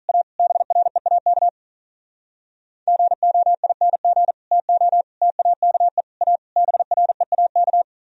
Morse code, heard as a keyed tone, about 42 wpm, 700 Hz. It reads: A BREAK GOING TO TAKE A BREAK
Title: Speed-Racing - Sets of 5 Words 35wpm